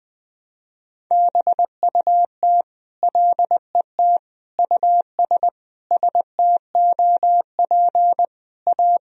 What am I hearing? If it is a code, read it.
Morse code, heard as a keyed tone, about 20 wpm, 700 Hz. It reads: BUT LET US STOP A
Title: Sentences from Top 1000 Words - Review Part 1 - 20wpm